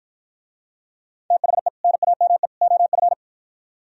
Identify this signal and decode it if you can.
Morse code, heard as a keyed tone, about 49 wpm, 700 Hz. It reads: THE NAME OF